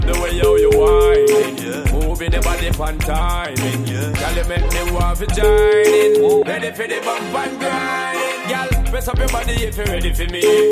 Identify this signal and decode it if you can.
Ringback tone (Europe, 425 Hz), with speech.